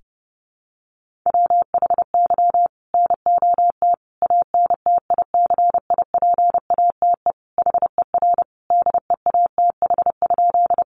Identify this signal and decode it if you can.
Morse code, heard as a keyed tone, about 30 wpm, 700 Hz. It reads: WHY NOT ANTICIPATE HER DEATH?
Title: Sentences from Top 1500 Words 30wpm